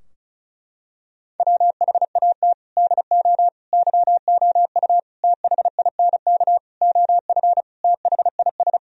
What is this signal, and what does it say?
Morse code, heard as a keyed tone, about 34 wpm, 700 Hz. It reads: WHAT DO YOU THINK OF THIS
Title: Sentences from Top 1000 Words - Review Part 2 - 35wpm